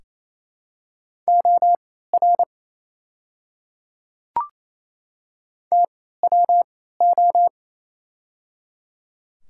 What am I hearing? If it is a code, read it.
Morse code, heard as a keyed tone, about 28 wpm, spaced as for 16 wpm, 700 Hz. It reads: OR TWO
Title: 3x Character Spacing - Top 100 Words - 28wpm